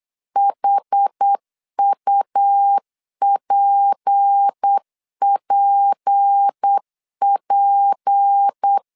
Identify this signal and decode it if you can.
Morse code, heard as a keyed tone, about 8 wpm, 800 Hz. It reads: HUPPP
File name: LEZIONI / 6 PKX0 / 6 P 8wpm 3025